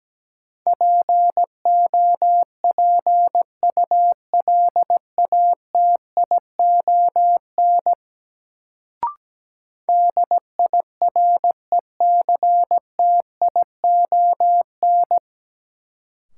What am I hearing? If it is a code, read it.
Morse code, heard as a keyed tone, about 17 wpm, 700 Hz. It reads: POPULATION DIRECTION
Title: Sentences from Top 1300 Words - New Word Drill Alt - 17wpm